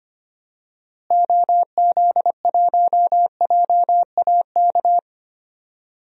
Morse code, heard as a keyed tone, about 25 wpm, 700 Hz. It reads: OZ1JAK
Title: DXCC-100 Call Signs - 6 Characters - Repeated 2x - 25wpm